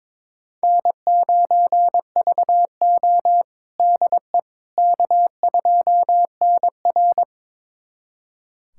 Morse code, heard as a keyed tone, about 22 wpm, 700 Hz. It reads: N9VO DE K2NR
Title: USA Callsign DE Callsign - 4 Characters - 22wpm